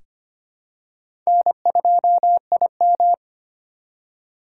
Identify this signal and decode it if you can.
Morse code, heard as a keyed tone, about 25 wpm, 700 Hz. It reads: N2IM
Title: USA Call Signs - 4x Character Spacing - 25wpm